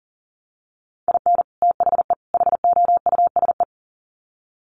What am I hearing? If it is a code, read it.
Morse code, heard as a keyed tone, about 38 wpm, 700 Hz. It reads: IN THE HOUSE
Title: Sets of 3 Words 40wpm